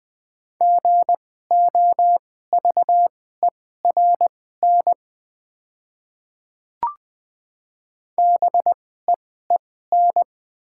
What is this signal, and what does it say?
Morse code, heard as a keyed tone, about 20 wpm, spaced as for 14 wpm, 690 Hz. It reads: GOVERN BEEN
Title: 2x Character Spacing - Top 500 Words - 20wpm